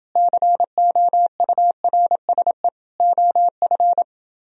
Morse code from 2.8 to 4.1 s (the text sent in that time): OF